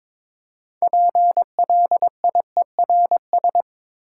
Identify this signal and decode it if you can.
Morse code, heard as a keyed tone, about 22 wpm, 700 Hz. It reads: PLIERS
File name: practice / 22wpm / Difficult Words with Hints - 22wpm